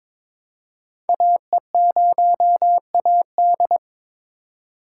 Morse code, heard as a keyed tone, about 22 wpm, 700 Hz. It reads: AE0AD
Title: Code Groups - 5 Letter:Numbers - 22wpm